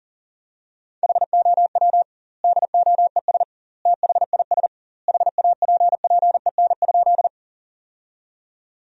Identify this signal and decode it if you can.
Morse code, heard as a keyed tone, about 40 wpm, 700 Hz. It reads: HOW DOES THIS HAPPEN?
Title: 2x Word-Spacing - Sentences from Top 400 Words - 40wpm